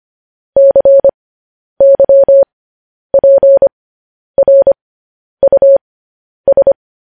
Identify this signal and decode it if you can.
Morse code, heard as a keyed tone, about 25 wpm, spaced as for 10 wpm, 550 Hz. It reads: CYPRUS
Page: news-headlines-in-morse-code-at-10-wpm